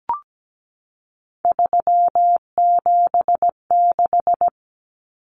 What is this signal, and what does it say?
Morse code, heard as a keyed tone, about 17 wpm, 700 Hz. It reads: 376
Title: Common Cut Numbers - 3-Digit 17wpm